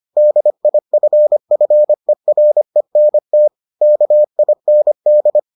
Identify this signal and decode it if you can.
Morse code, heard as a keyed tone, about 25 wpm, 600 Hz. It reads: DIFFERENT KIND